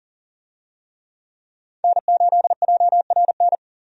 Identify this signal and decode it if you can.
Morse code, heard as a keyed tone, about 40 wpm, 700 Hz. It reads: N8JRN
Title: USA Call Signs - 5 Characters - 40wpm